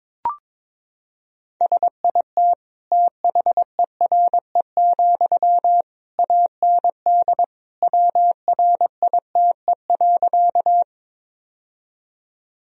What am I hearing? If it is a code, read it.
Morse code, heard as a keyed tone, about 21 wpm, 700 Hz. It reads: SIT THERE, AND WRITE.
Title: Sentences from Top 1000 Words - Review Part 1 - 22wpm